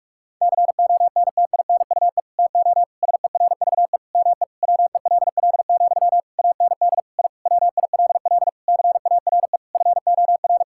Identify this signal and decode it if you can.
Morse code, heard as a keyed tone, about 46 wpm, 700 Hz. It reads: CONTINUE TO SERVE ME WELL, AND I WILL CARE FOR
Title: Sentences from Top 1000 Words - Review Part 6 - 45wpm